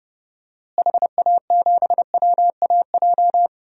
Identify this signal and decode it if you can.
Morse code, heard as a keyed tone, about 30 wpm, 700 Hz. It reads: HA7WAJ
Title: DXCC-100 Call Signs - 6 Characters - Repeated 2x - 30wpm